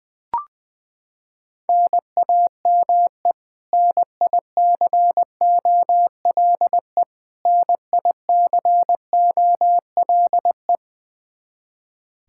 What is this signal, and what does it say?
Morse code, heard as a keyed tone, about 20 wpm, 700 Hz. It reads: NAME NICOLE NICOLE
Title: Practice QSO - Exchange 1 - Beginning - 20wpm